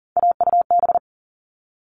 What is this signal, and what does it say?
Morse code, heard as a keyed tone, about 40 wpm, 700 Hz. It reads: AUB
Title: All Cut Numbers - 3-Digit 40wpm